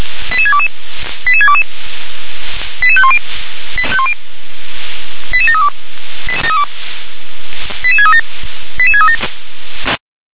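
Five-tone selective call (ZVEI), heard as quick runs of five tones, tones of about 70 ms. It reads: 88622 88622 88510 88732 88637 88637
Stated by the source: ZVEI five-tone selective call 88513 received by SDR.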